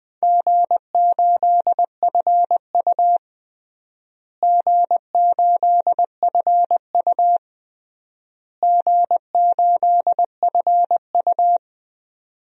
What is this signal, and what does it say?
Morse code, heard as a keyed tone, about 21 wpm, 700 Hz. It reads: G8FU G8FU G8FU
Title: DXCC-100 Call Signs - 4 Characters - Repeated 3x - 20wpm